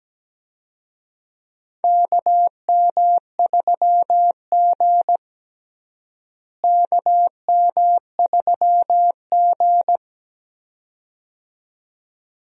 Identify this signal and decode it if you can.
Morse code, heard as a keyed tone, about 17 wpm, 700 Hz. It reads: KM3G KM3G
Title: USA Call Signs - 4 Characters - Repeated 2x - 17wpm